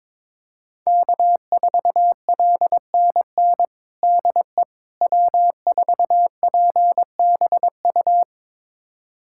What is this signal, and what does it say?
Morse code, heard as a keyed tone, about 22 wpm, 700 Hz. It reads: K4LNN DE W4PBU
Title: USA Callsign DE Callsign - 5 Characters - 22wpm